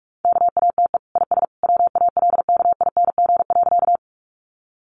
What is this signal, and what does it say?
Morse code, heard as a keyed tone, about 44 wpm, 700 Hz. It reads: KATE IS WALKING.
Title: Easy Phrases 45wpm